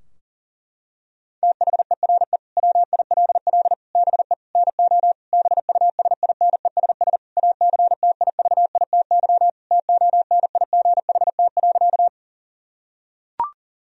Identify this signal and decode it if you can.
Morse code, heard as a keyed tone, about 37 wpm, 700 Hz. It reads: THERE WILL BE NO BUSINESS ACTIVITY TONIGHT.
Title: Sentences from Top 1000 Words - Review Part 4 - 40wpm